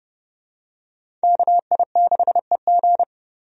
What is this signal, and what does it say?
Morse code, heard as a keyed tone, about 30 wpm, 700 Hz. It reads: KI6EG
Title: USA Call Signs - 5 Characters - 30wpm